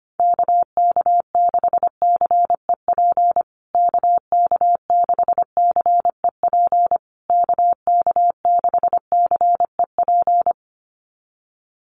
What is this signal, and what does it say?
Morse code, heard as a keyed tone, about 25 wpm, 700 Hz. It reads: KK6CEP KK6CEP KK6CEP
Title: USA Call Signs Repeated 3x 25wpm